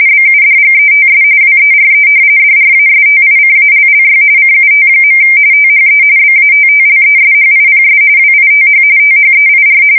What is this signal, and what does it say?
RTTY, about 45.45 baud. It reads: BUDGETS WERE DRASTICALLY CUT. THE BATTLESHIPS WITH THEIR
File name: fsk170 / battleshipnj